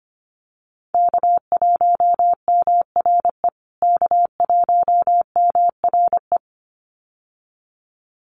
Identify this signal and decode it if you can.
Morse code, heard as a keyed tone, about 25 wpm, 700 Hz. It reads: K1MRE K1MRE
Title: USA Call Signs Repeated 2x 25wpm